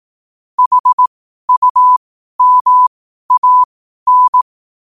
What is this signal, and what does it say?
Morse code, heard as a keyed tone, about 18 wpm, spaced as for 12 wpm, 1000 Hz. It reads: HUMAN
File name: qotdfiles / 12wpm